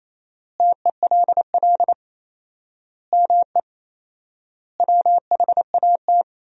Morse code, heard as a keyed tone, about 28 wpm, 700 Hz. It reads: TELL ME WHAT
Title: 4x Word-Spacing - Sentences from Top 200 Words - 28wpm